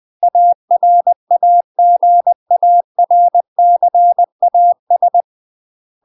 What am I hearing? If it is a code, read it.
Morse code, heard as a keyed tone, about 20 wpm, 700 Hz. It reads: ARAGARCAS
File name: UK MASTER QSOs TRAINING / 20WPM / UK MASTER QSOs 0014